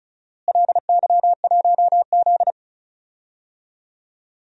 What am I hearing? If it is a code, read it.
Morse code, heard as a keyed tone, about 34 wpm, 700 Hz. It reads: LY1Z
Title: DXCC-100 Call Signs - 4 Characters - Repeated 2x - 35wpm